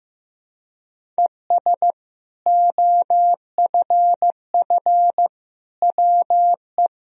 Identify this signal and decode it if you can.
Morse code, heard as a keyed tone, about 16 wpm, 700 Hz. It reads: ES OFF WE
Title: Sentences from Top 200 Words and Abbreviations 15wpm